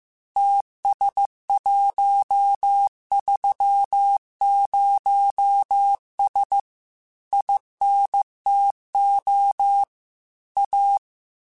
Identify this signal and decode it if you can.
Morse code, heard as a keyed tone, about 15 wpm, 800 Hz. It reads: TS130S INTO A
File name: QSOs / QSOs-mp3s / WD0BGP 1@10